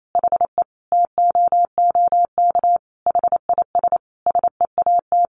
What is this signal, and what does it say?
Morse code, heard as a keyed tone, about 28 wpm, 700 Hz. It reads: HE TOOK HIS SEAT
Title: Sentences from Top 1500 Words 28wpm